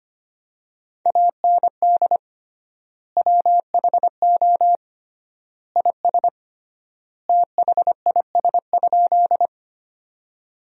Morse code, heard as a keyed tone, about 25 wpm, 700 Hz. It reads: AND WHO IS THIS?
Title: 3x Word-Spacing - Sentences from Top 100 Words - 25wpm